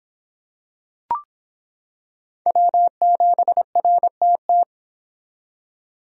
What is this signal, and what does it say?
Morse code, heard as a keyed tone, about 25 wpm, 700 Hz. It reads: W7RTT